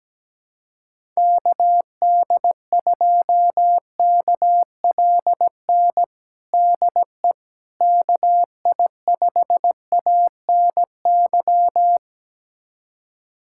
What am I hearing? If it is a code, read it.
Morse code, heard as a keyed tone, about 17 wpm, 700 Hz. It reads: KD2KLN DE KI5ANY
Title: USA Callsign DE Callsign - 6 Characters - 17wpm